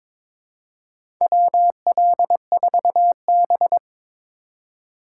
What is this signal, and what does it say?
Morse code, heard as a keyed tone, about 22 wpm, 700 Hz. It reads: WL4B
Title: USA Call Signs - 4 Characters - 4x Character Spacing - 22wpm